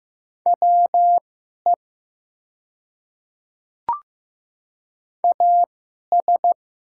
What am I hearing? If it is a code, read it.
Morse code, heard as a keyed tone, about 15 wpm, spaced as for 11 wpm, 700 Hz. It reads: WE AS
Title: 2x Character Spacing - Top 100 Words - 15wpm